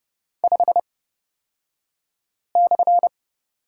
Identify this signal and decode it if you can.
Morse code, heard as a keyed tone, about 31 wpm, spaced as for 5 wpm, 700 Hz. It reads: 5/
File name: Lesson 38d - Letter q - Two Characters 30wpm